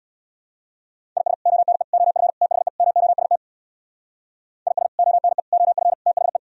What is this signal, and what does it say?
Morse code, heard as a keyed tone, about 64 wpm, 700 Hz. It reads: IS ONE OF THE MOST IS ONE OF THE